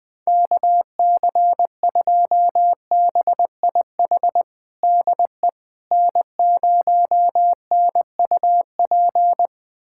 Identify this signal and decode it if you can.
Morse code, heard as a keyed tone, about 20 wpm, 700 Hz. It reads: KC2BIH DE N0NUP
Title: USA Callsign DE Callsign 20wpm